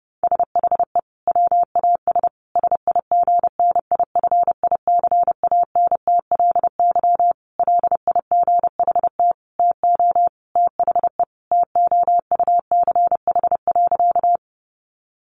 Morse code, heard as a keyed tone, about 30 wpm, 700 Hz. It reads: SHE WAS SIGNIFICANTLY LIGHT TO THE TOUCH.
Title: Sentences from Top 1500 Words 30wpm